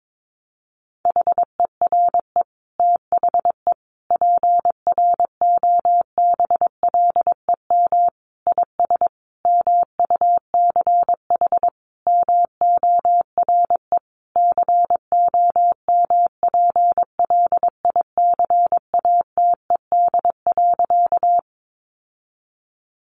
Morse code, heard as a keyed tone, about 22 wpm, 700 Hz. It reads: HERE THE PROBLEM IS MUCH MORE COMPLICATED.
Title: Sentences from Top 1200 Words 22wpm